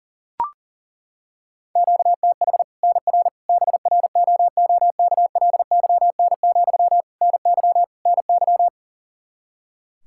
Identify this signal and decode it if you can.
Morse code, heard as a keyed tone, about 40 wpm, 700 Hz. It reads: QTH NR BROOKLYN, NY NY
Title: Practice QSO - Exchange 1 - Beginning - 40wpm